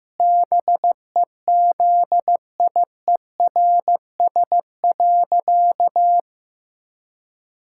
Morse code, heard as a keyed tone, about 15 wpm, 700 Hz. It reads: BEZIERS.
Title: Pangram Challenge v2 - 15wpm